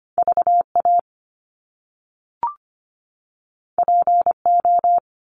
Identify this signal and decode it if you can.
Morse code, heard as a keyed tone, about 25 wpm, 700 Hz. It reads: VA PO
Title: Top Two Letter Combinations in Words 25wpm